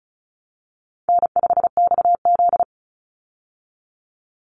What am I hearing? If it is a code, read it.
Morse code, heard as a keyed tone, about 34 wpm, 700 Hz. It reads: N5XZ
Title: Easy Call Signs Repeated 3x - 35wpm